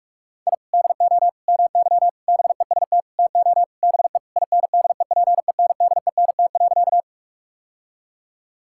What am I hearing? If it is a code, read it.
Morse code, heard as a keyed tone, about 46 wpm, 700 Hz. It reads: I DO MY BEST TO BE INDEPENDENT.
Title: Sentences from Top 1000 Words - Review Part 6 - 45wpm